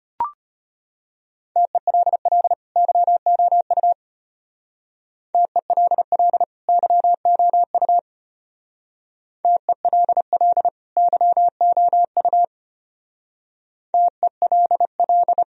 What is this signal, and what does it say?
Morse code, heard as a keyed tone, about 33 wpm, 700 Hz. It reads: TELL YOU TELL YOU TELL YOU TELL